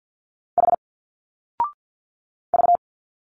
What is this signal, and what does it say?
Morse code, heard as a keyed tone, about 48 wpm, 700 Hz. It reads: H V
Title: Single Letters 50wpm